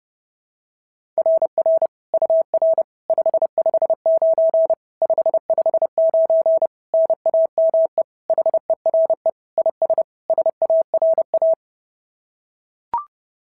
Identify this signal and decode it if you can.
Morse code, heard as a keyed tone, about 28 wpm, 660 Hz. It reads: RR UR 559 559 NAME HERE IS SARA
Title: SOTA QSO Practice 30wpm